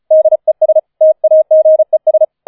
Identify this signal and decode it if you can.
Morse code, heard as a keyed tone, about 34 wpm, 630 Hz. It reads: DES TAGES
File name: 004 Vierter Streich 35wpm